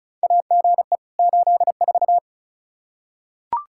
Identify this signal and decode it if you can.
Morse code, heard as a keyed tone, about 34 wpm, 700 Hz. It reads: AGE 84
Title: Practice QSO - Exchange 3 - More Info - 35wpm